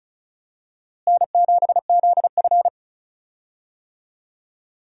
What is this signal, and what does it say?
Morse code, heard as a keyed tone, about 36 wpm, 700 Hz. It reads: N7ZF
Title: USA Call Signs - 4 Characters - Repeated 2x - 35wpm